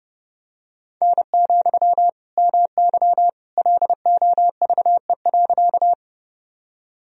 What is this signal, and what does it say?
Morse code, heard as a keyed tone, about 31 wpm, 700 Hz. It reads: N, MY LOVE.
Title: Sentences from Top 400 Words and Abbreviations 30wpm